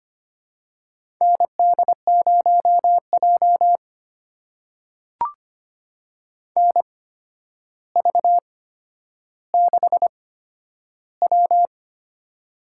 Morse code, heard as a keyed tone, about 25 wpm, spaced as for 11 wpm, 700 Hz. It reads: ND0J NV6W
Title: USA Call Signs - 4 Characters - 8x Character Spacing - 25wpm